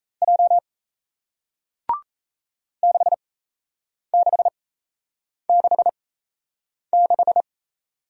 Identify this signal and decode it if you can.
Morse code, heard as a keyed tone, about 38 wpm, spaced as for 8 wpm, 700 Hz. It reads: J 6666